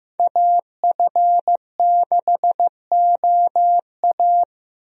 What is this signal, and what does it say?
Morse code, heard as a keyed tone, about 15 wpm, 700 Hz. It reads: AF6OA
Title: USA Call Signs - 5 Characters - Repeated 2x - 15wpm